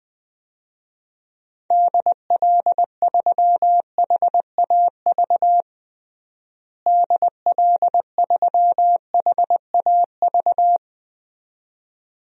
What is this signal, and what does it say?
Morse code, heard as a keyed tone, about 20 wpm, 700 Hz. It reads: DL3HAV DL3HAV
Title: DXCC-100 Call Signs - 6 Characters - Repeated 2x - 20wpm